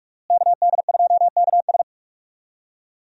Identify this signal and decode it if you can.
Morse code, heard as a keyed tone, about 42 wpm, 700 Hz. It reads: KD2KS
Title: USA Call Signs - 5 Characters - Repeated 3x - 45wpm